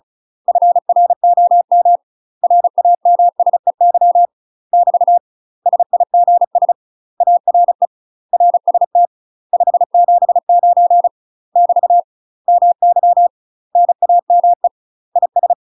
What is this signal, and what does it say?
Morse code, heard as a keyed tone, about 36 wpm, 700 Hz. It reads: FROM RAMSEY <BT> SIGS ARE RST 579 <BT> MY NAME IS
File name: UK MASTER QSOs TRAINING / 35WPM / UK MASTER QSOs 0011